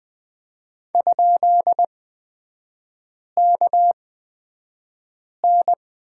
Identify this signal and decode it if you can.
Morse code, heard as a keyed tone, about 20 wpm, spaced as for 5 wpm, 700 Hz. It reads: ?KN
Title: 255 Lesson 41e - Period - Three Characters 20wpm